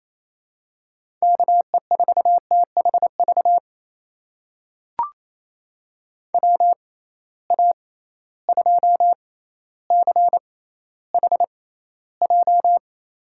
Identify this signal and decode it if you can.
Morse code, heard as a keyed tone, about 27 wpm, spaced as for 14 wpm, 700 Hz. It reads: KE4THV WA2CHJ
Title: USA Call Signs - 6 Characters - 6x Character Spacing - 28wpm